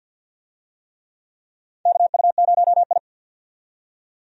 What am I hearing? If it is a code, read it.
Morse code, heard as a keyed tone, about 48 wpm, 700 Hz. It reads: KU0I
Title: USA Call Signs - 4 Characters - 50wpm